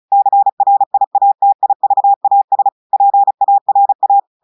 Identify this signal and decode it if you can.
Morse code, heard as a keyed tone, about 36 wpm, 800 Hz. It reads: CRIATIVAS PARA
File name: portugais texte en clair 2 vitesse 35 mots minutes